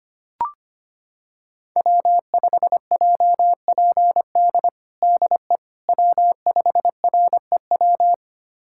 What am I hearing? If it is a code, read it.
Morse code, heard as a keyed tone, about 25 wpm, 700 Hz. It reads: W5JPD DE W5REW
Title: USA Callsign DE Callsign - 5 Characters - 25wpm